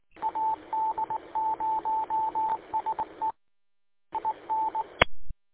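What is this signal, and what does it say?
Morse code, heard as a keyed tone, about 19 wpm, 880 Hz. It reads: AD0SE IN
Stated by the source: Morse code from the audio archive of station w8um-145230.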